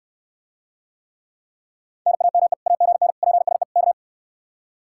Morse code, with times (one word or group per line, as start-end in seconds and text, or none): 2.1-2.5 NAME
2.7-3.1 AGN
3.2-3.6 PSE
3.8-3.9 K